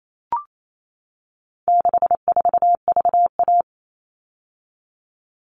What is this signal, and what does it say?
Morse code, heard as a keyed tone, about 28 wpm, 700 Hz. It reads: 64VA
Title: All Cut Numbers - 4-Digit 28wpm